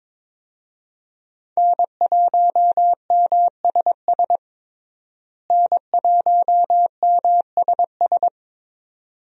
Morse code, heard as a keyed tone, about 22 wpm, 700 Hz. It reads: N1MSS N1MSS
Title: USA Call Signs - 5 Characters - Repeated 2x - 22wpm